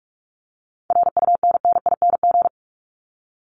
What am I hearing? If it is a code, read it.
Morse code, heard as a keyed tone, about 44 wpm, 700 Hz. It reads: RUNNING
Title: ING Suffix - Top 100 Words with Base Words 45wpm